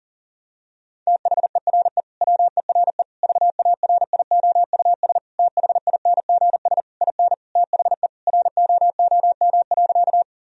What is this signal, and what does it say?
Morse code, heard as a keyed tone, about 40 wpm, 700 Hz. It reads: THERE WERE VARIOUS THINGS IN THE ROOM.
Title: Sentences from Top 1000 Words - Review Part 4 - 40wpm